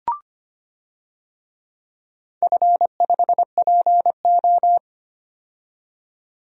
Morse code, heard as a keyed tone, about 25 wpm, 700 Hz. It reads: F5PO
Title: DXCC-100 Call Signs - 4 Characters - 25wpm